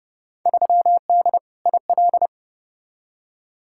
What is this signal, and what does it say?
Morse code, heard as a keyed tone, about 30 wpm, 700 Hz. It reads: 3D IL
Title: Field Day Exchanges Encore - 30wpm